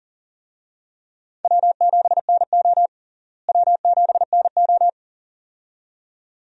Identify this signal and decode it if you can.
Morse code, heard as a keyed tone, about 40 wpm, 700 Hz. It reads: W7NO W7NO
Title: USA Call Signs - 4 Characters - Repeated 2x - 40wpm